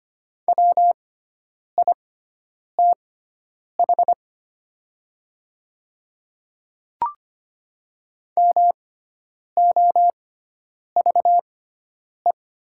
Morse code, heard as a keyed tone, about 25 wpm, spaced as for 9 wpm, 700 Hz. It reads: WITH MOVE
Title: 6x Character Spacing - Top 500 Words - 25wpm